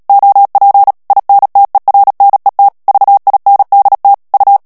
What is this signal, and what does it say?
Morse code, heard as a keyed tone, about 37 wpm, 800 Hz. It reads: OP INTERNET VINDT U
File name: morse mp3 / HOLLANDAIS texte clair 2 vitesse 35 mots minute